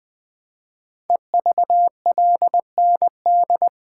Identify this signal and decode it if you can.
Morse code, heard as a keyed tone, about 20 wpm, 700 Hz. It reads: EVLND
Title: Code Groups - 5 Letter:Numbers - 20wpm